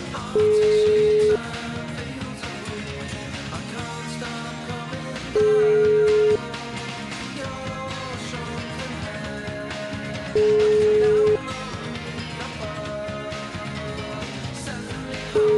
Ringback tone (Europe, 425 Hz), with noise.